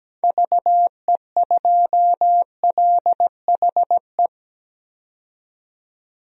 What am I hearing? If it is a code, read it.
Morse code, heard as a keyed tone, about 17 wpm, 700 Hz. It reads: VE2LHE
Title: DXCC-100 Call Signs - 6 Characters - 17wpm